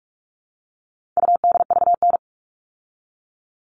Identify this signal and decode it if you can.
Morse code, heard as a keyed tone, about 44 wpm, 700 Hz. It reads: UDVN
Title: All Cut Numbers - 4-Digit 45wpm